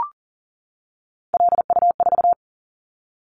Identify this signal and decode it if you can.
Morse code, heard as a keyed tone, about 42 wpm, 700 Hz. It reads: LU4
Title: Callsign Prefixes for Top 50 DXCC - 40wpm